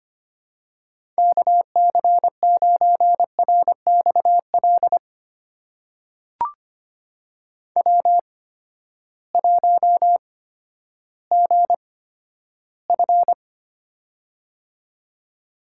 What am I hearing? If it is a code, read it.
Morse code, heard as a keyed tone, about 25 wpm, spaced as for 12 wpm, 700 Hz. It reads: KC9RXL W1GF
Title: USA Call Signs - 8x Character Spacing - 25wpm